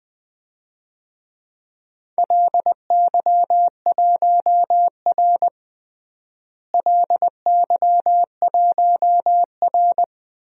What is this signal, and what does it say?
Morse code, heard as a keyed tone, about 20 wpm, 700 Hz. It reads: LY1R LY1R
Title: DXCC-100 Call Signs - 4 Characters - Repeated 2x - 20wpm